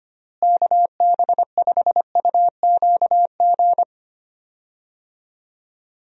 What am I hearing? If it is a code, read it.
Morse code, heard as a keyed tone, about 25 wpm, 700 Hz. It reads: KB5UQG
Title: USA Call Signs - 6 Characters - Repeated 2x - 25wpm